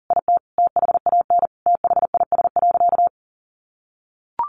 Morse code, heard as a keyed tone, about 40 wpm, 700 Hz. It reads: IT THAN THIS.
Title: Sentences from Top 100 Words 40wpm